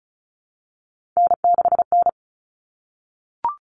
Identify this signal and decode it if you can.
Morse code, heard as a keyed tone, about 34 wpm, 700 Hz. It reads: N6N
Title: All Cut Numbers - 3-Digit 35wpm